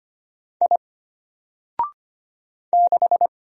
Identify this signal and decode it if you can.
Morse code, heard as a keyed tone, about 25 wpm, 700 Hz. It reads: I 6
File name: Single-Letter-Number-25wpm